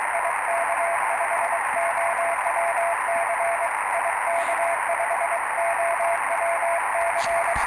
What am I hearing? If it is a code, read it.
Morse code, heard as a keyed tone, about 23 wpm, 680 Hz, with radio noise. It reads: IK5OWC IK5OWC
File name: attachments / 09.05.2019 14060 CQ IK5OWC R326M R309